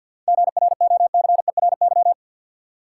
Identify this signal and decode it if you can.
Morse code, heard as a keyed tone, about 50 wpm, 700 Hz. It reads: GROCERY
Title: Seven Letter Words - 50wpm